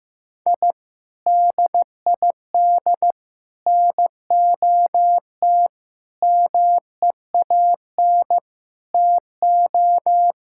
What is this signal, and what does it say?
Morse code, heard as a keyed tone, about 16 wpm, 700 Hz. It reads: I DID NOT MEAN TO
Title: Sentences from Top 1000 Words - Review Part 6 - 15wpm